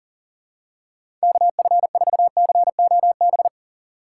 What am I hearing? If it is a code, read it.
Morse code, heard as a keyed tone, about 40 wpm, 700 Hz. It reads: KF4COB
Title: USA Call Signs - 6 Characters - 6x Character Spacing - 40wpm